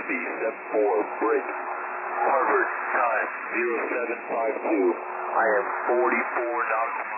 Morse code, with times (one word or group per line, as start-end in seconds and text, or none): none